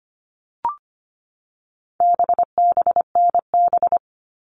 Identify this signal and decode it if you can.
Morse code, heard as a keyed tone, about 25 wpm, 700 Hz. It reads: BBNB
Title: All Cut Numbers - 4-Digit 25wpm